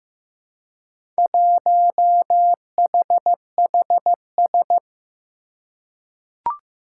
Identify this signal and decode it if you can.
Morse code, heard as a keyed tone, about 15 wpm, 700 Hz. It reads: 1HHS
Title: Four Challenging Letter-Number Combinations 15wpm